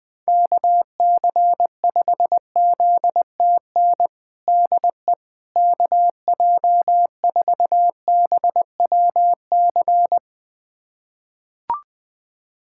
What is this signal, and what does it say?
Morse code, heard as a keyed tone, about 20 wpm, 700 Hz. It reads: KC5ZTN DE KJ4BWC
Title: USA Callsign DE Callsign 20wpm